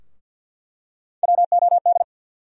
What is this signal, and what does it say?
Morse code, heard as a keyed tone, about 49 wpm, 700 Hz. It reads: WOD
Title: USA Call Sign Postfixes - 3 Letters 50wpm